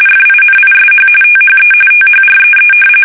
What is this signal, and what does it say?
RTTY, about 45.45 baud. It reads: FINISH THE THREE
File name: fsk850 / usnavy1